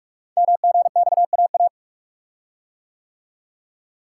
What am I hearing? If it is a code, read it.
Morse code, heard as a keyed tone, about 44 wpm, 700 Hz. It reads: MGXAA
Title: Code Groups - 5 Letters - 45wpm